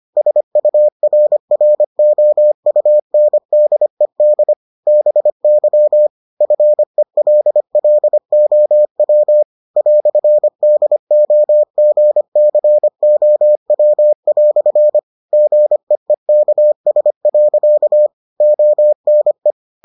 Morse code, heard as a keyed tone, about 25 wpm, 600 Hz. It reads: SURROUNDED BY FELLOW "DOGCOW" GEEKS. ONE